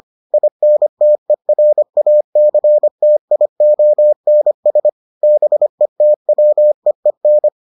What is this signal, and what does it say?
Morse code, heard as a keyed tone, about 25 wpm, 600 Hz. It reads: INTERACTIONS BETWEEN